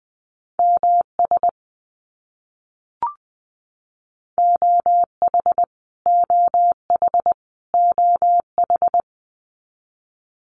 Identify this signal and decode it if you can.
Morse code, heard as a keyed tone, about 20 wpm, 700 Hz. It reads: MS OH OH OH
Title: US States Repeated 3x 20wpm